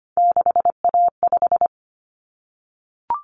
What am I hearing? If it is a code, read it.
Morse code, heard as a keyed tone, about 25 wpm, 700 Hz. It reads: 6A5
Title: Common Cut Numbers - 3-Digit 25wpm